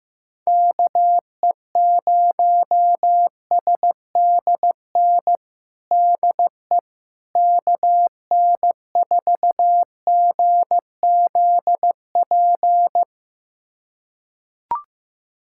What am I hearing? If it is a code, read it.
Morse code, heard as a keyed tone, about 15 wpm, 700 Hz. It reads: KE0SDN DE KN4GZP
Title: USA Callsign DE Callsign - 6 Characters - 15wpm